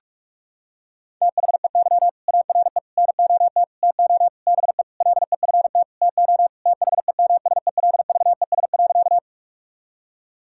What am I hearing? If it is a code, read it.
Morse code, heard as a keyed tone, about 46 wpm, 700 Hz. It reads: THEY ARE NOT TO BE LEFT TO THEMSELVES.
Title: Sentences from Top 1000 Words - Review Part 5 - 45wpm